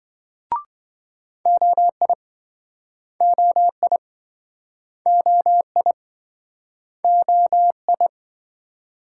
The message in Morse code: OI OI OI OI

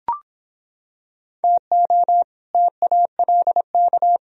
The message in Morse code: TO TALK

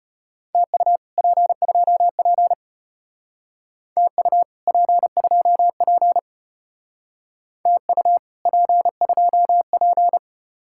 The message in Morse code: TU P2P TU P2P TU P2P

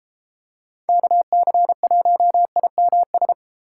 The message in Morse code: KC1IMS

